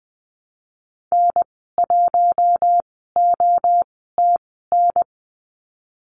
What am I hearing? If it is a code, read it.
Morse code, heard as a keyed tone, about 20 wpm, spaced as for 15 wpm, 700 Hz. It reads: N1OTN
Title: 042 Lesson 8g - Number 1 - Call Signs 20wpm